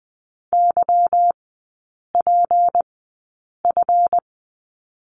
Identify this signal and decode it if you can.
Morse code, heard as a keyed tone, about 20 wpm, spaced as for 8 wpm, 700 Hz. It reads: YPF
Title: 178 Lesson 29e - Slash - Three Characters 20wpm